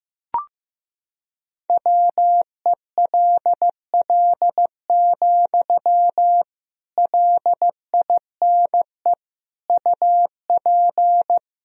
Morse code, heard as a keyed tone, about 15 wpm, 700 Hz. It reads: WELL, LINE UP
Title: Sentences from Top 1000 Words - Review Part 1 - 15wpm